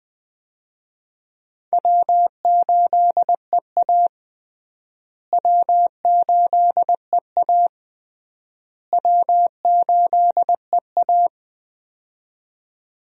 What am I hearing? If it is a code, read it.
Morse code, heard as a keyed tone, about 20 wpm, 700 Hz. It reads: W8EA W8EA W8EA
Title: USA Call Signs - 4 Characters - Repeated 3x - 20wpm